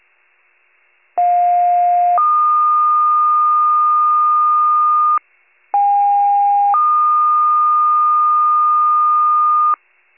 Two-tone page, heard as a one-second tone, then a three-second tone, about 710 then 1220 Hz, about 810 then 1220 Hz. It is heard from 1.2 to 9.7 s.
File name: pager / 73 RESCUE / wcalert